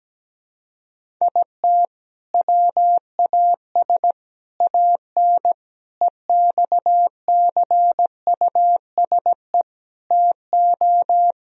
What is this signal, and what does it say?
Morse code, heard as a keyed tone, about 17 wpm, 700 Hz. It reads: IT WAS AN EXCUSE TO
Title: Sentences from Top 1000 Words - Review Part 1 - 17wpm